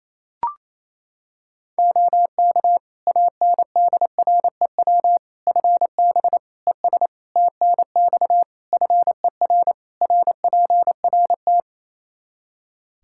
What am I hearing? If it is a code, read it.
Morse code, heard as a keyed tone, about 28 wpm, 700 Hz. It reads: OK ANDREW FB ES TNX FER RPRT
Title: Practice QSO - Exchange 2 - Info - 28wpm